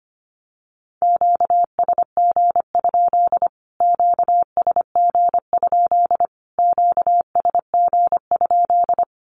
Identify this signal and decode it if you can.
Morse code, heard as a keyed tone, about 25 wpm, 700 Hz. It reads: QSG? QSG? QSG?
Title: Q-Codes Repeated 3x 25wpm